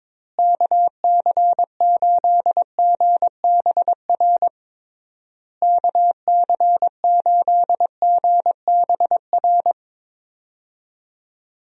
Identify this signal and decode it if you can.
Morse code, heard as a keyed tone, about 22 wpm, 700 Hz. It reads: KC8GBR KC8GBR
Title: USA Call Signs - 6 Characters - Repeated 3x - 22wpm